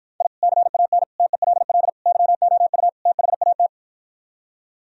Morse code, heard as a keyed tone, about 52 wpm, 700 Hz. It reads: I CAN TELL YOU THAT